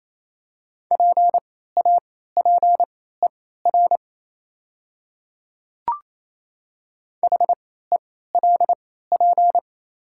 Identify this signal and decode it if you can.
Morse code, heard as a keyed tone, about 28 wpm, spaced as for 16 wpm, 700 Hz. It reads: PAPER HELP
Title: 3x Character Spacing - Top 500 Words - 28wpm